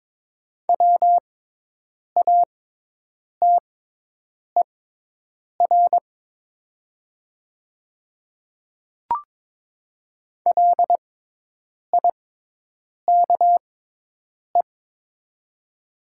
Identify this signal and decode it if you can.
Morse code, heard as a keyed tone, about 22 wpm, spaced as for 8 wpm, 700 Hz. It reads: WATER LIKE
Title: 6x Character Spacing - Top 100 Words - 22wpm